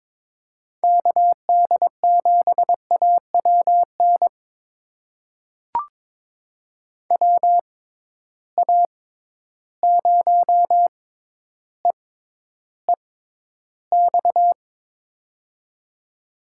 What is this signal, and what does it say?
Morse code, heard as a keyed tone, about 22 wpm, spaced as for 11 wpm, 700 Hz. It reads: KD7AWN WA0EEX
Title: USA Call Signs - 6 Characters - 6x Character Spacing - 22wpm